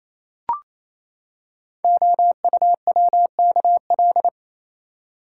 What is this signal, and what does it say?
Morse code, heard as a keyed tone, about 29 wpm, 700 Hz. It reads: OUWKL
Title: Code Groups - 5 Letters - 28wpm